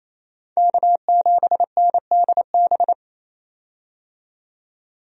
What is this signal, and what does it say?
Morse code, heard as a keyed tone, about 28 wpm, 700 Hz. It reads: K7NDB
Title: USA Call Signs - 5 Characters - 28wpm